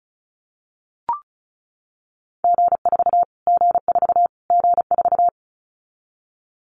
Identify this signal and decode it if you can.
Morse code, heard as a keyed tone, about 34 wpm, 700 Hz. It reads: G4 G4 G4
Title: Callsign Prefixes for Top 25 DXCC - Repeated 3x - 35wpm